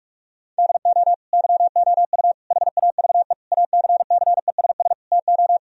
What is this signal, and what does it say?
Morse code, heard as a keyed tone, about 46 wpm, 700 Hz. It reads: DO YOU HAVE ACCESS TO
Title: Sentences from Top 1000 Words - Review Part 4 - 45wpm